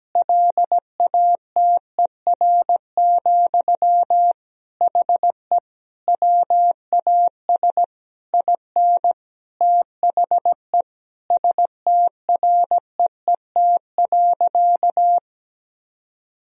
Morse code, heard as a keyed tone, about 17 wpm, 700 Hz. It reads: LATER, HE WAS IN THE STREET.
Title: Sentences from Top 1000 Words - Review Part 4 - 17wpm